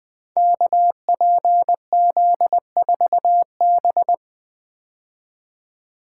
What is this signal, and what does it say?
Morse code, heard as a keyed tone, about 20 wpm, 700 Hz. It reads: KPZ4B
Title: Code Groups - 5 Letter:Numbers - 20wpm